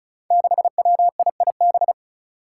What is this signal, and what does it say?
Morse code, heard as a keyed tone, about 34 wpm, 700 Hz. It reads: 6WIIB
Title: Five Challenging Letter-Number Combinations 35wpm